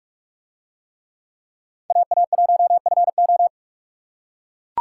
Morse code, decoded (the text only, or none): AA1FO